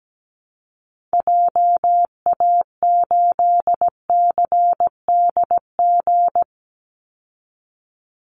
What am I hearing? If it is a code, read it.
Morse code, heard as a keyed tone, about 17 wpm, 700 Hz. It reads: JA8CDG
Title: Call Signs Repeated 2x - 17wpm